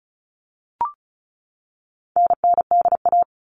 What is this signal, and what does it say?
Morse code, heard as a keyed tone, about 34 wpm, 700 Hz. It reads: NNDA
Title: All Cut Numbers - 4-Digit 35wpm